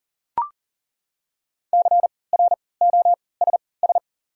Morse code, heard as a keyed tone, about 40 wpm, spaced as for 23 wpm, 690 Hz. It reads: CROSS
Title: 3x Character Spacing - Top 500 Words - 40wpm